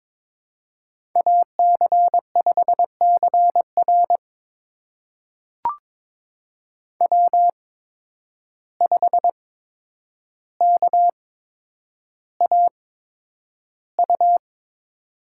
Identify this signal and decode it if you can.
Morse code, heard as a keyed tone, about 22 wpm, spaced as for 9 wpm, 700 Hz. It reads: AC5CR W5KAU
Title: USA Call Signs - 5 Characters - 8x Character Spacing - 22wpm